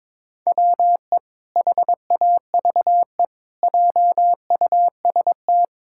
Morse code, heard as a keyed tone, about 21 wpm, 700 Hz. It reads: WE HAVE JUST
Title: Pangram Challenge v2 - 22wpm